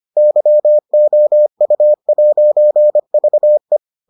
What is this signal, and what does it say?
Morse code, heard as a keyed tone, about 25 wpm, 600 Hz. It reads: YOU'VE